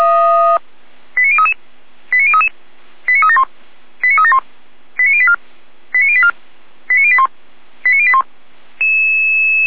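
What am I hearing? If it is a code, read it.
Five-tone selective call (ZVEI), heard as quick runs of five tones, tones of about 80 ms. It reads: 89033 89033 89371 89371 89084 89084 89081 89081
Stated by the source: ZVEI five-tone selective call 89337, received by SDR, from the feuerwehr-orlamuende.org audio server.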